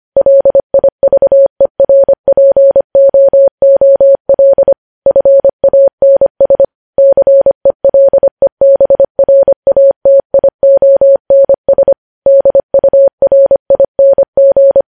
Morse code, heard as a keyed tone, about 25 wpm, 550 Hz. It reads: LIVERPOOL FANS CELEBRATIONS DURING
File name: podcast / News headlines-1x-25x25